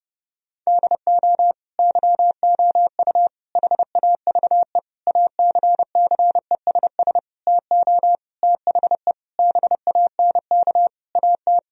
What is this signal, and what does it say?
Morse code, heard as a keyed tone, about 30 wpm, 700 Hz. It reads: DO YOU HAVE ACCESS TO THE BANK AT
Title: Sentences from Top 1000 Words - Review Part 4 - 30wpm